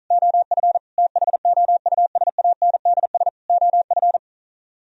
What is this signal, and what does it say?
Morse code, heard as a keyed tone, about 46 wpm, 700 Hz. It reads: OF THOUSANDS OF